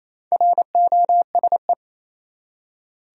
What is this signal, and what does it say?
Morse code, heard as a keyed tone, about 28 wpm, 700 Hz. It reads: ROSE
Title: USA Common Names - Max 4 Letters - 28wpm